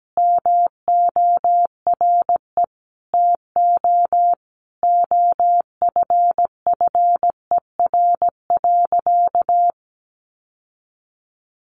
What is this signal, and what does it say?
Morse code, heard as a keyed tone, about 17 wpm, 700 Hz. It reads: MORE TO OFFER.
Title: Sentences from Top 1100 Words 17wpm